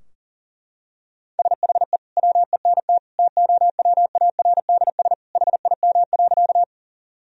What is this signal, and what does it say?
Morse code, heard as a keyed tone, about 40 wpm, 700 Hz. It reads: SHE WENT TOWARDS HIM.
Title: Sentences from Top 1000 Words - Review Part 2 - 40wpm